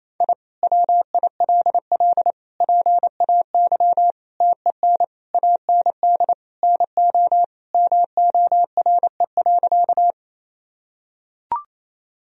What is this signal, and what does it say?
Morse code, heard as a keyed tone, about 28 wpm, 700 Hz. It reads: I WILL PAY TEN AND NO MORE.
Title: Sentences from Top 1000 Words - Review Part 2 - 28wpm